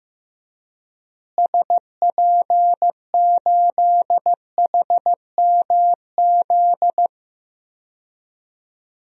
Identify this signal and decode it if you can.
Morse code, heard as a keyed tone, about 15 wpm, 700 Hz. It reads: SP8HMZ
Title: DXCC-100 Call Signs - 6 Characters - Repeated 3x - 15wpm